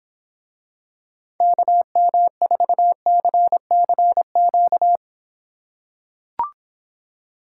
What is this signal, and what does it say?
Morse code, heard as a keyed tone, about 26 wpm, 700 Hz. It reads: KM4CCQ